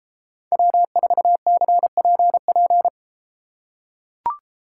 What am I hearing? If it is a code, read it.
Morse code, heard as a keyed tone, about 33 wpm, 700 Hz. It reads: W4CPP